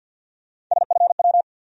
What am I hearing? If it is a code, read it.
Morse code, heard as a keyed tone, about 52 wpm, 700 Hz. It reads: SFW
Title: USA Call Sign Postfixes - 3 Letters 50wpm